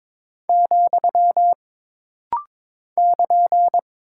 Morse code, heard as a keyed tone, about 22 wpm, 700 Hz. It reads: , <KN>
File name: practice / 22wpm / Single Punctuation-Prosign - Rapid-Fire - 22wpm